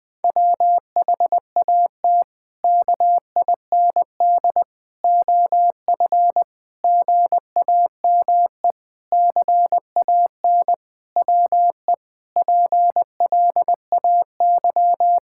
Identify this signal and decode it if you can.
Morse code, heard as a keyed tone, about 20 wpm, 700 Hz. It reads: WHAT KIND OF GAME CAN WE PLAY
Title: Sentences from Top 1000 Words - Review Part 6 - 20wpm